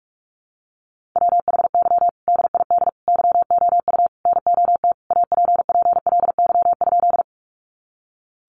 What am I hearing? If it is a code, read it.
Morse code, heard as a keyed tone, about 44 wpm, 700 Hz. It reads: WHY DID YOU NOT APPLY?
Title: Sentences from Top 1200 Words 45wpm